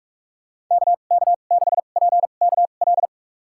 Morse code, heard as a keyed tone, about 46 wpm, spaced as for 33 wpm, 700 Hz. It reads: KK6PKL
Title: USA Call Signs - 6 Characters - 2x Character Spacing - 45wpm